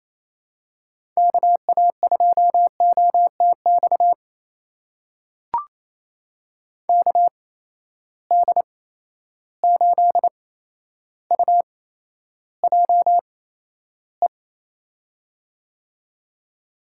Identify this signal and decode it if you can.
Morse code, heard as a keyed tone, about 28 wpm, spaced as for 12 wpm, 700 Hz. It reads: KA2OTX KD8UJE
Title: USA Call Signs - 6 Characters - 8x Character Spacing - 28wpm